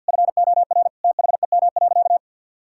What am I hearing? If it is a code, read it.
Morse code, heard as a keyed tone, about 46 wpm, 700 Hz. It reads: FOR THEM.
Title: Sentences from Top 1600 Words 50wpm